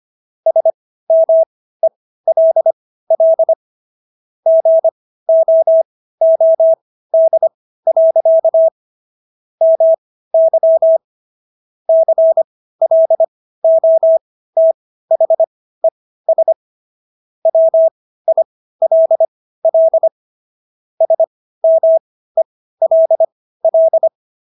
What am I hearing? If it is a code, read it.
Morse code, heard as a keyed tone, about 25 wpm, spaced as for 15 wpm, 650 Hz. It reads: SMELL GOOD. MY CLOTHES WILL SMELL